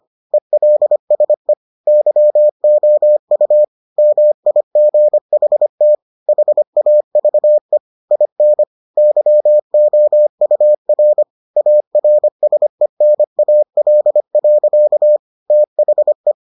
Morse code, heard as a keyed tone, about 25 wpm, 600 Hz. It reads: ELSE YOU MIGHT HAVE IN YOUR ARSENAL. THE